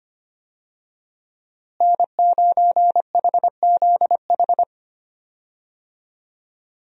Morse code, heard as a keyed tone, about 25 wpm, 700 Hz. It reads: N9HZH
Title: USA Call Signs - 5 Characters - Repeated 3x - 25wpm